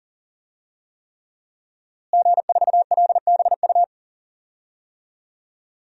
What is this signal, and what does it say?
Morse code, heard as a keyed tone, about 40 wpm, 700 Hz. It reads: G4LBU
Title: DXCC-100 Call Signs - 5 Characters - Repeated 3x - 40wpm